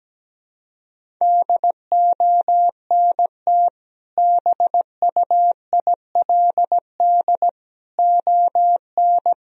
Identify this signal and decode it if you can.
Morse code, heard as a keyed tone, about 17 wpm, 700 Hz. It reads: DONT BUILD ON
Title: Sentences from Top 1000 Words - Review Part 1 - 17wpm